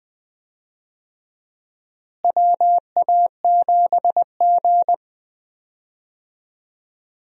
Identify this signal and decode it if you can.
Morse code, heard as a keyed tone, about 20 wpm, 700 Hz. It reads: WA7G
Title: USA Call Signs - 4 Characters - Repeated 2x - 20wpm